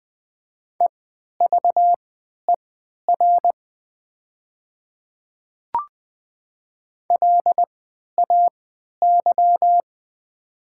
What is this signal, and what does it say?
Morse code, heard as a keyed tone, about 20 wpm, spaced as for 11 wpm, 700 Hz. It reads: EVER LAY
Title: 3x Character Spacing - Top 500 Words - 20wpm